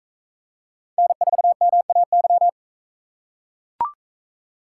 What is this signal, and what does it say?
Morse code, heard as a keyed tone, about 39 wpm, 700 Hz. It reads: N4MAY